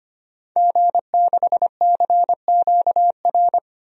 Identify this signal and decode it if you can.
Morse code, heard as a keyed tone, about 25 wpm, 700 Hz. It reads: G6CQR
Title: DXCC-100 Call Signs - 5 Characters - 25wpm